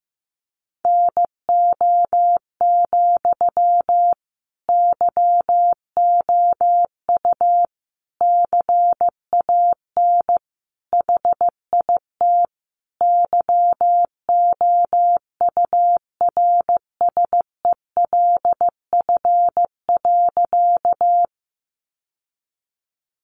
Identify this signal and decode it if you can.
Morse code, heard as a keyed tone, about 14 wpm, 700 Hz. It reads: NO, YOU CAN HIT YOURSELF.
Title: Sentences from Top 1200 Words 15wpm